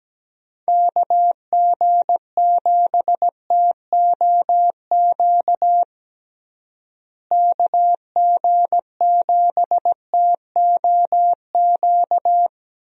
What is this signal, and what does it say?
Morse code, heard as a keyed tone, about 17 wpm, 700 Hz. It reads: KG7TOQ KG7TOQ
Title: USA Call Signs - 6 Characters - Repeated 3x - 17wpm